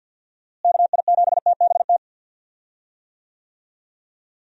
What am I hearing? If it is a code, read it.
Morse code, heard as a keyed tone, about 52 wpm, 700 Hz. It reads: KI7TBT
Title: USA Call Signs - 6 Characters - Repeated 3x - 50wpm